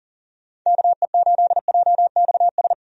Morse code, heard as a keyed tone, about 40 wpm, 700 Hz. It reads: KE8JXS
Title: USA Call Signs - 6 Characters - 40wpm